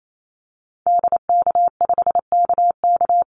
Morse code, heard as a keyed tone, about 28 wpm, 700 Hz. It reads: DK5KK
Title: Call Signs Repeated 2x - 28wpm